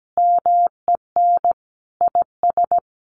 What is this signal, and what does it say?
Morse code, heard as a keyed tone, about 17 wpm, 700 Hz. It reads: MEN IS
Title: Sentences from Top 1100 Words 17wpm